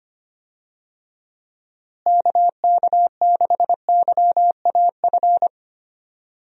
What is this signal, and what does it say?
Morse code, heard as a keyed tone, about 25 wpm, 700 Hz. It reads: KK6YAF
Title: USA Call Signs - 6 Characters - Repeated 3x - 25wpm